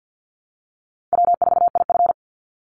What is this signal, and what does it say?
Morse code, heard as a keyed tone, about 50 wpm, 700 Hz. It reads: W4IF
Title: Call Signs - 50wpm